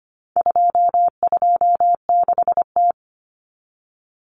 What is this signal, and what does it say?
Morse code, heard as a keyed tone, about 25 wpm, 700 Hz. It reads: 226T
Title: Common Cut Numbers - 4-Digit 25wpm